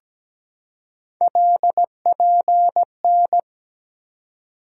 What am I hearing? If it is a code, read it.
Morse code, heard as a keyed tone, about 17 wpm, 700 Hz. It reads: LPN
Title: K1USN SST QSO Practice 17wpm